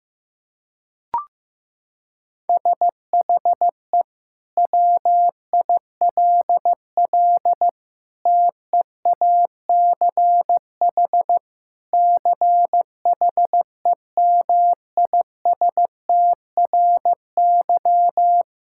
Morse code, heard as a keyed tone, about 16 wpm, 700 Hz. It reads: SHE WILL TEACH CHEMISTRY
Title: Sentences from Top 1000 Words - Review Part 6 - 15wpm